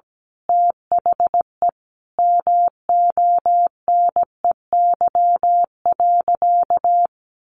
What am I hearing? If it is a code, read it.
Morse code, heard as a keyed tone, about 17 wpm, 700 Hz. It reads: THE MONEY.
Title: Sentences from Top 1100 Words 17wpm